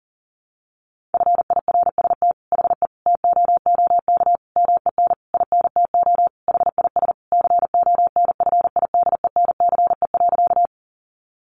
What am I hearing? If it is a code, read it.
Morse code, heard as a keyed tone, about 40 wpm, 700 Hz. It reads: FIRST HE TOOK MEN INTO HIS CONFIDENCE.
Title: Sentences from Top 1500 Words 40wpm